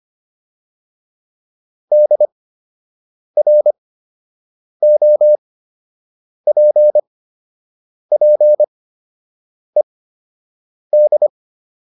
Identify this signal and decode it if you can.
Morse code, heard as a keyed tone, about 25 wpm, spaced as for 7 wpm, 600 Hz. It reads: DROPPED